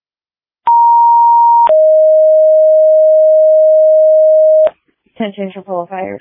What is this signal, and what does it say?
Two-tone page, heard as a one-second tone, then a three-second tone, about 940 then 620 Hz, with speech. It is heard from 0.7 to 4.7 s.